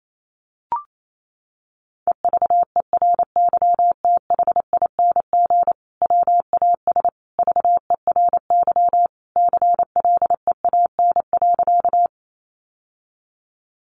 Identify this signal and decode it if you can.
Morse code, heard as a keyed tone, about 28 wpm, 700 Hz. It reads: EVERYTHING WAS VERY CLEAN.
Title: Sentences from Top 1100 Words 28wpm